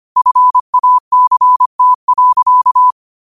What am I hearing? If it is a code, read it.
Morse code, heard as a keyed tone, about 25 wpm, 1000 Hz. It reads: FACT.